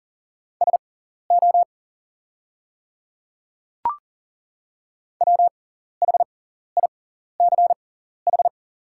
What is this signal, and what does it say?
Morse code, heard as a keyed tone, about 40 wpm, spaced as for 14 wpm, 700 Hz. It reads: SO WHICH
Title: 6x Character Spacing - Top 100 Words - 40wpm